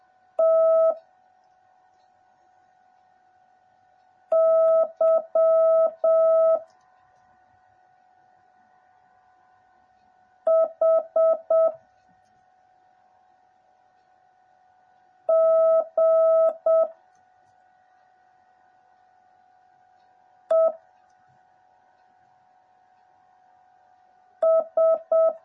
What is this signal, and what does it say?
Morse code, heard as a keyed tone, about 7 wpm, 650 Hz. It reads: T Y H G E S
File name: jh8cbh / 2021cw / cw09a